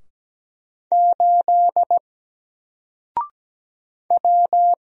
Morse code, heard as a keyed tone, about 17 wpm, 700 Hz. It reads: 8 W
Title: Single Letter-Number 17wpm